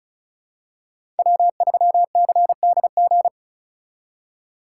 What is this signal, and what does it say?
Morse code, heard as a keyed tone, about 36 wpm, 690 Hz. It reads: W3CDG
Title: USA Call Signs - 5 Characters - 3x Character Spacing - 35wpm